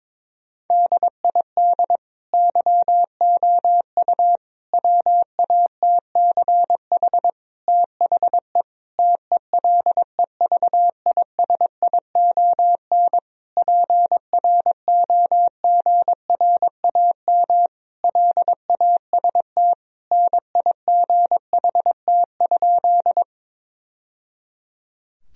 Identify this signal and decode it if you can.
Morse code, heard as a keyed tone, about 22 wpm, 700 Hz. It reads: DID YOU WATCH THE TELEVISION PROGRAM LAST NIGHT?
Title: Sentences from Top 1000 Words - Review Part 4 - 22wpm